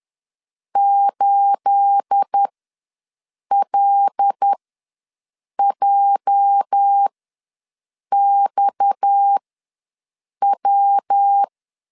Morse code, heard as a keyed tone, about 10 wpm, spaced as for 6 wpm, 800 Hz. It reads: 8LJXW
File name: LEZIONI / 9 28 ws 10wpm 4124